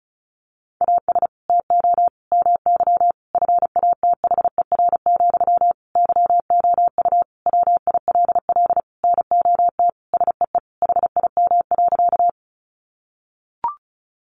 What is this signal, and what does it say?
Morse code, heard as a keyed tone, about 34 wpm, 700 Hz. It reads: AS TO MY FATHER, YOU WILL NOT SEE HIM.
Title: Sentences from Top 600 Words 35wpm